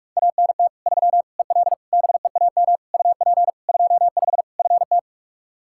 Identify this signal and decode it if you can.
Morse code, heard as a keyed tone, about 41 wpm, 700 Hz. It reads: ANT 3 EL BEAM UP 25 FT
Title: K1USN SST QSO Practice 45wpm